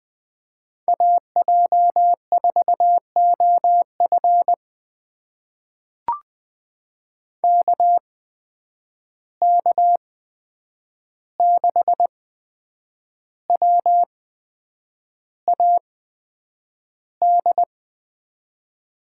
Morse code, heard as a keyed tone, about 20 wpm, spaced as for 8 wpm, 700 Hz. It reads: AJ4OF KK6WAD
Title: USA Call Signs - 8x Character Spacing - 20wpm